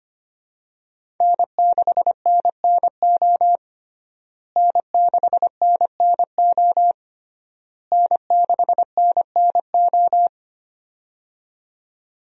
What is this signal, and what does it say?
Morse code, heard as a keyed tone, about 25 wpm, 700 Hz. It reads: N6NNO N6NNO N6NNO
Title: USA Call Signs - 5 Characters - Repeated 3x - 25wpm